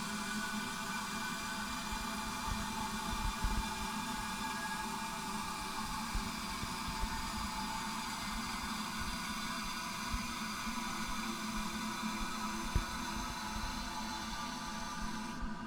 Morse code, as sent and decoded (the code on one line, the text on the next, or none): none